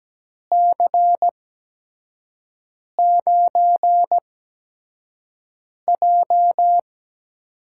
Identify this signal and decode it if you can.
Morse code, heard as a keyed tone, about 17 wpm, spaced as for 5 wpm, 700 Hz. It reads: C9J
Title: USA Call Signs - 6 Characters - 8x Character Spacing - 17wpm